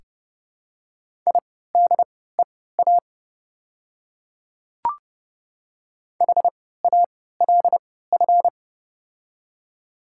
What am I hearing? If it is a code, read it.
Morse code, heard as a keyed tone, about 30 wpm, spaced as for 17 wpm, 690 Hz. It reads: IDEA HALF
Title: 3x Character Spacing - Top 500 Words - 30wpm